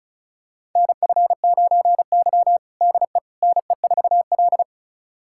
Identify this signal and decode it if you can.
Morse code, heard as a keyed tone, about 36 wpm, 700 Hz. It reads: NF9Y DE NE4L
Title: USA Callsign DE Callsign - 4 Characters - 35wpm